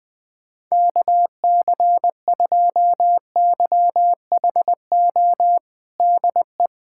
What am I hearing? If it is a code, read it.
Morse code, heard as a keyed tone, about 20 wpm, 700 Hz. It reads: KC2YHO DE
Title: USA Callsign DE Callsign - 6 Characters - 20wpm